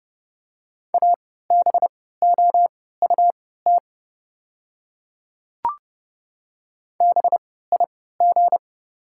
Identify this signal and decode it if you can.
Morse code, heard as a keyed tone, about 30 wpm, spaced as for 17 wpm, 700 Hz. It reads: ABOUT BIG
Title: 3x Character Spacing - Top 200 Words - 30wpm